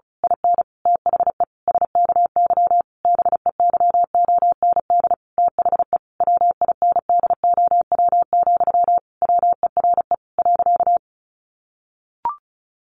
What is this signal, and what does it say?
Morse code, heard as a keyed tone, about 34 wpm, 700 Hz. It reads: IN THE SKY BEYOND THE WINDOW, WERE .
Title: A Sound Of Thunder 35wpm